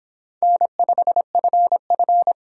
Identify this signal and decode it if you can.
Morse code, heard as a keyed tone, about 26 wpm, 700 Hz. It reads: N5FF